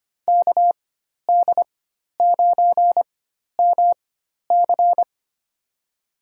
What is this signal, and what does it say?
Morse code, heard as a keyed tone, about 25 wpm, spaced as for 12 wpm, 700 Hz. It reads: KD9MC
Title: USA Call Signs - 5 Characters - 4x Character Spacing - 25wpm